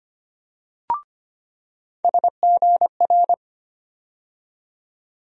Morse code, heard as a keyed tone, about 25 wpm, 700 Hz. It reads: SGR
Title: USA Call Sign Postfixes - 3 Letters 25wpm